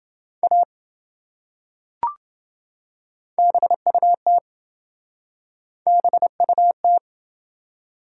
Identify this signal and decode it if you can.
Morse code, heard as a keyed tone, about 27 wpm, 700 Hz. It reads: A BUT BUT